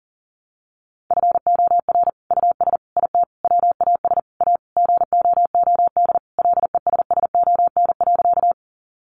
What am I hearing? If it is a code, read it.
Morse code, heard as a keyed tone, about 40 wpm, 700 Hz. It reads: FOR US IT WAS A GOOD LESSON.
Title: Sentences from Top 1200 Words 40wpm